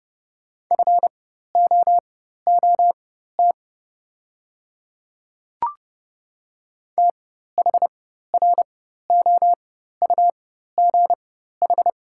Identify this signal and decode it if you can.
Morse code, heard as a keyed tone, about 32 wpm, spaced as for 14 wpm, 700 Hz. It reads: FOOT THROUGH
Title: 4x Character Spacing - Top 500 Words - 30wpm